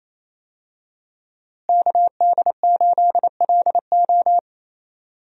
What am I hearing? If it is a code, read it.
Morse code, heard as a keyed tone, about 28 wpm, 700 Hz. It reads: KD8LO
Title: USA Call Signs - 5 Characters - Repeated 3x - 28wpm